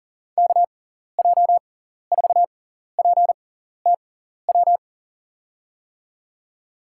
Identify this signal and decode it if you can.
Morse code, heard as a keyed tone, about 40 wpm, spaced as for 14 wpm, 700 Hz. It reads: KJ4PTW
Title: USA Call Signs - 6 Characters - 6x Character Spacing - 40wpm